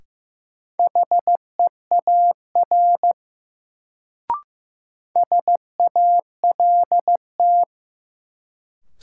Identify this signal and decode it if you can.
Morse code, heard as a keyed tone, about 15 wpm, 700 Hz. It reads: HEAR SALT
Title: Sets of 5 Words - New Word Drill - 15wpm